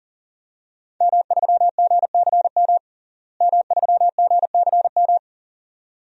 Morse code, heard as a keyed tone, about 40 wpm, 700 Hz. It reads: M3GCM M3GCM
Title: DXCC-100 Call Signs - 5 Characters - Repeated 2x - 40wpm